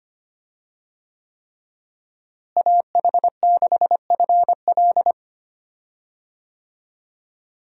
Morse code, heard as a keyed tone, about 25 wpm, 700 Hz. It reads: AH6FL
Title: USA Call Signs - 5 Characters - Repeated 3x - 25wpm